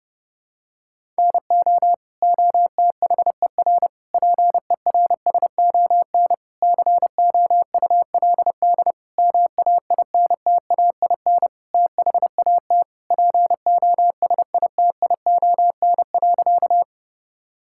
Morse code, heard as a keyed tone, about 30 wpm, 700 Hz. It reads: NO OTHER PERSON COULD MAINTAIN THAT POSITION.
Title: Sentences from Top 1000 Words - Review Part 5 - 30wpm